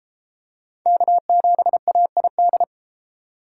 Morse code, heard as a keyed tone, about 32 wpm, 700 Hz. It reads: K7AID